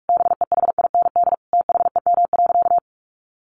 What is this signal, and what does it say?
Morse code, heard as a keyed tone, about 44 wpm, 700 Hz. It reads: BEHIND THEM.
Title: Sentences from Top 1400 Words 45wpm